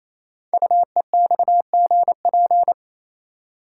Morse code, heard as a keyed tone, about 27 wpm, 700 Hz. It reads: UEXGP
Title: Code Groups - 5 Letters - 28wpm